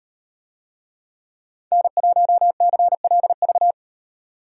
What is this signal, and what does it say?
Morse code, heard as a keyed tone, about 38 wpm, 700 Hz. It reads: N1CLV